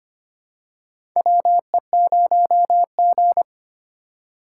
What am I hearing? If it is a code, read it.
Morse code, heard as a keyed tone, about 25 wpm, 700 Hz. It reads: WE0G
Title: USA Call Signs - 4 Characters - 2x Character Spacing - 25wpm